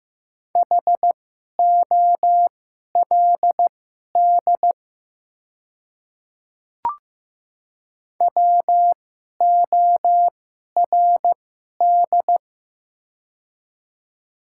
Morse code, heard as a keyed tone, about 15 wpm, spaced as for 11 wpm, 690 Hz. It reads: HOLD WORD
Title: 2x Character Spacing - Top 500 Words - 15wpm